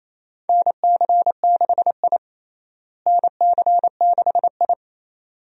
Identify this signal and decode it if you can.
Morse code, heard as a keyed tone, about 28 wpm, 700 Hz. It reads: NC6I NC6I
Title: USA Call Signs - 4 Characters - Repeated 2x - 28wpm